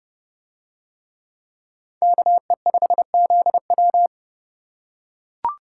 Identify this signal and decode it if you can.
Morse code, heard as a keyed tone, about 29 wpm, 700 Hz. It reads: KE5ZW